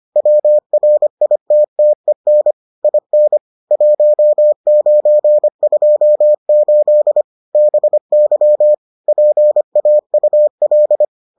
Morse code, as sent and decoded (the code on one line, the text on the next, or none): .-- .-. .. - - . -. / .. -. / .---- ----. ..--- ---.. / -... -.-- / .--. .- ..- .-..
WRITTEN IN 1928 BY PAUL